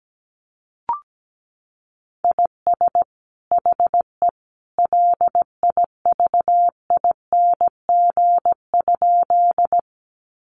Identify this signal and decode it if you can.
Morse code, heard as a keyed tone, about 17 wpm, 700 Hz. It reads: IS HE LIVING?
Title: Sentences from Top 1100 Words 17wpm